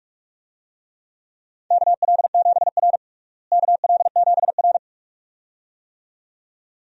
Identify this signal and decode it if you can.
Morse code, heard as a keyed tone, about 46 wpm, 700 Hz. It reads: KL7R KL7R
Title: DXCC-100 Call Signs - 4 Characters - Repeated 2x - 45wpm